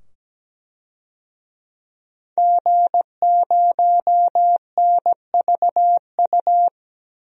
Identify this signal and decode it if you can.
Morse code, heard as a keyed tone, about 17 wpm, 700 Hz. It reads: G0NVU
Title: DXCC-100 Call Signs - 5 Characters - 17wpm